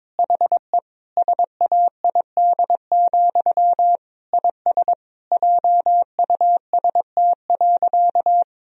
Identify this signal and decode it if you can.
Morse code, heard as a keyed tone, about 22 wpm, 700 Hz. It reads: HE SAID, IS JUST.
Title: Sentences from Top 1000 Words - Review Part 4 - 22wpm